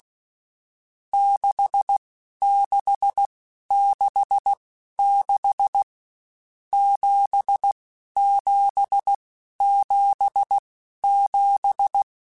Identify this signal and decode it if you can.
Morse code, heard as a keyed tone, about 16 wpm, spaced as for 11 wpm, 800 Hz. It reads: 6666 7777
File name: Misc / Misc-mp3s / Numb @15